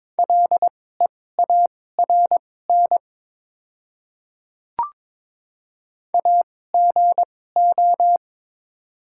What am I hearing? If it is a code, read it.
Morse code, heard as a keyed tone, about 23 wpm, spaced as for 16 wpm, 700 Hz. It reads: LEARN AGO
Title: 2x Character Spacing - Top 500 Words - 22wpm